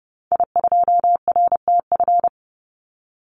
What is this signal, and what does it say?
Morse code, heard as a keyed tone, about 30 wpm, 700 Hz. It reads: I2RTF
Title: Call Signs - 30wpm